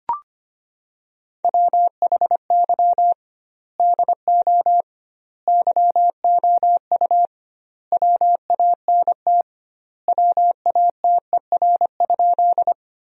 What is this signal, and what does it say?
Morse code, heard as a keyed tone, about 25 wpm, 700 Hz. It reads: WHY DO YOU WANT WATER?
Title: 2x Word-Spacing - Sentences from Top 200 Words - 25wpm